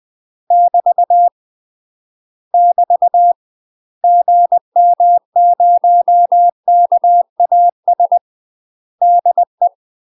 Morse code, heard as a keyed tone, about 20 wpm, 700 Hz. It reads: <BT> <BT> GM0KAS DE
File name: UK MASTER QSOs TRAINING / 20WPM / UK MASTER QSOs 0072